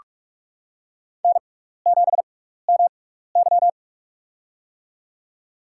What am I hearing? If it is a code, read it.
Morse code, heard as a keyed tone, about 46 wpm, spaced as for 15 wpm, 700 Hz. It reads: N7MY
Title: USA Call Signs - 4 Characters - 6x Character Spacing - 45wpm